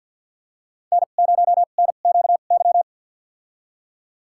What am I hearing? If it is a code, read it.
Morse code, heard as a keyed tone, about 51 wpm, spaced as for 36 wpm, 700 Hz. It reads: N0NQY
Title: USA Call Signs - 5 Characters - 2x Character Spacing - 50wpm